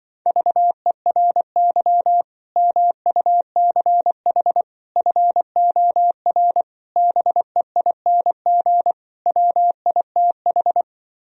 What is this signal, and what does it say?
Morse code, heard as a keyed tone, about 25 wpm, 700 Hz. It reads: VERY MUCH FOR BEING WITH
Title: Speed-Racing - Sets of 5 Words 20wpm